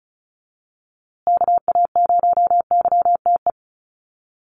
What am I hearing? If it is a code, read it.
Morse code, heard as a keyed tone, about 34 wpm, 700 Hz. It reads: KA0YTE
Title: USA Call Signs Repeated 2x 35wpm